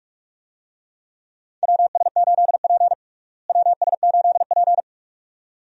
Morse code, heard as a keyed tone, about 42 wpm, 700 Hz. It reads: WS8P WS8P
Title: USA Call Signs - 4 Characters - Repeated 2x - 45wpm